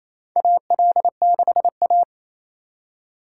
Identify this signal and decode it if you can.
Morse code, heard as a keyed tone, about 28 wpm, 700 Hz. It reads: AL6A
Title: USA Call Signs - 4 Characters - 6x Character Spacing - 28wpm